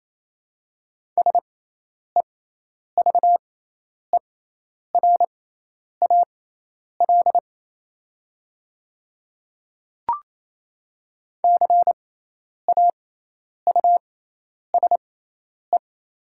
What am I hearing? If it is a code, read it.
Morse code, heard as a keyed tone, about 28 wpm, spaced as for 10 wpm, 700 Hz. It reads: SEVERAL CAUSE
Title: 6x Character Spacing - Top 500 Words - 28wpm